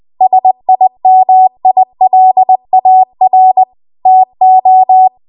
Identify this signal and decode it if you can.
Morse code, heard as a keyed tone, about 20 wpm, 750 Hz. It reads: SIMILAR TO
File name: Archive / 20 WPM / 20WPM